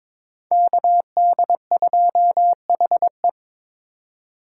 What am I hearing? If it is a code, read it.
Morse code, heard as a keyed tone, about 22 wpm, 700 Hz. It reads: KD2HE
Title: K1USN SST QSO Practice - 22wpm